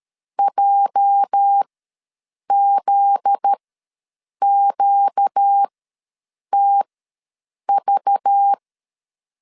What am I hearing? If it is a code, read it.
Morse code, heard as a keyed tone, about 13 wpm, spaced as for 7 wpm, 800 Hz. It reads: JZQTV
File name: LEZIONI / 9 286 ws 12wpm 3351